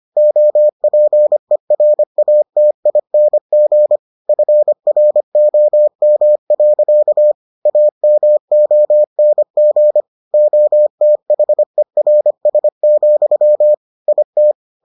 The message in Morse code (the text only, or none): OPERATING FROM. AMONG OTHERS, IT